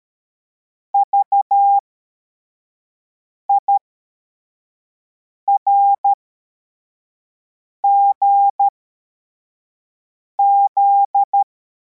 Morse code, heard as a keyed tone, about 13 wpm, spaced as for 4 wpm, 800 Hz. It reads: VIRGZ